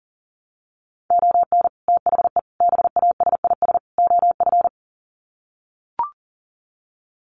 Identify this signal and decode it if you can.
Morse code, heard as a keyed tone, about 40 wpm, 700 Hz. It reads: ON THE BASIS OF
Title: Sets of 4 Words 40wpm